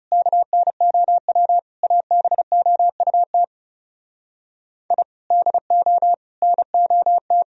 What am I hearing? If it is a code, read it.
Morse code, heard as a keyed tone, about 38 wpm, 700 Hz. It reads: KNOW ABOUT I DO NOT